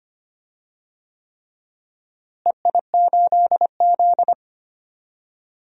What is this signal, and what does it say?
Morse code, heard as a keyed tone, about 25 wpm, 700 Hz. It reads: EI8Z
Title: DXCC-100 Call Signs - 4 Characters - Repeated 3x - 25wpm